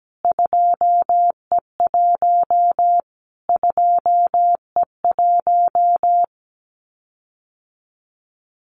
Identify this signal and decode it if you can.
Morse code, heard as a keyed tone, about 17 wpm, 700 Hz. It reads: 2E1 2E1
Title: Callsign Prefixes for Top 25 DXCC - Repeated 3x - 17wpm